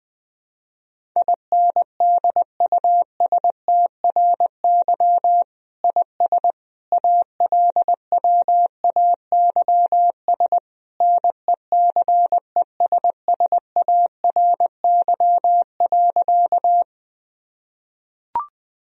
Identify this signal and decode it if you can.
Morse code, heard as a keyed tone, about 20 wpm, 700 Hz. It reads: INDUSTRY IS ALWAYS NECESSARY.
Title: Sentences from Top 1000 Words - Review Part 4 - 20wpm